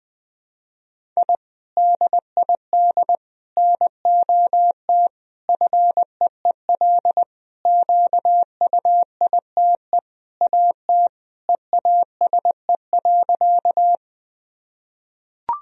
Morse code, heard as a keyed tone, about 20 wpm, 700 Hz. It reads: I DID NOT FEEL QUITE AT EASE.
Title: Sentences from Top 1000 Words - Review Part 1 - 20wpm